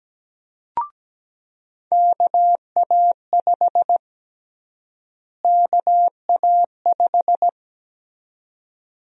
Morse code, heard as a keyed tone, about 17 wpm, 700 Hz. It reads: KA5 KA5
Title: USA Call Sign Prefixes - Repeated 2x - 17wpm